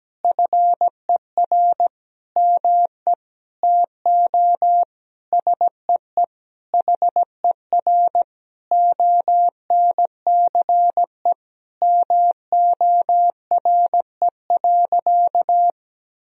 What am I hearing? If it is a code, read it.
Morse code, heard as a keyed tone, about 17 wpm, 700 Hz. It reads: FER ME TO SEE HER ONCE MORE.
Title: Sentences from Top 500 Words and Abbreviations 17wpm